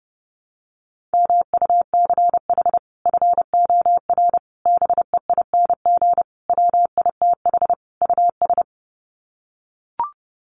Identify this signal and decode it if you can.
Morse code, heard as a keyed tone, about 30 wpm, 700 Hz. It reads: MUCH FOR BEING WITH US
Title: Sets of 5 Words 30wpm